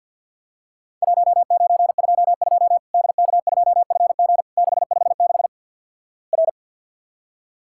Morse code, heard as a keyed tone, about 50 wpm, 700 Hz. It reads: 1922 DK2FG 656 R
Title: Worked All Europe DX Contest - QTC Exchange - 50wpm